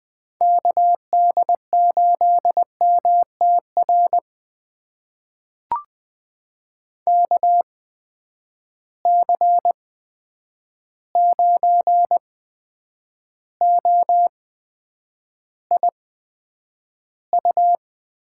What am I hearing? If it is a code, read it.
Morse code, heard as a keyed tone, about 20 wpm, spaced as for 9 wpm, 700 Hz. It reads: KD8MTR KC9OIU
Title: USA Call Signs - 6 Characters - 8x Character Spacing - 20wpm